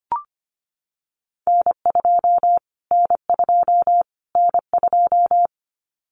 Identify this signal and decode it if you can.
Morse code, heard as a keyed tone, about 25 wpm, 700 Hz. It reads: N2 N2 N2
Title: Callsign Prefixes for Top 25 DXCC - Repeated 3x - 25wpm